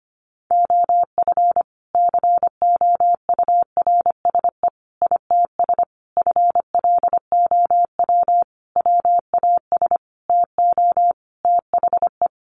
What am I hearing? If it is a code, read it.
Morse code, heard as a keyed tone, about 25 wpm, 700 Hz. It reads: OF COURSE ITS FLOW WAS TO THE
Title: Sentences from Top 1200 Words 25wpm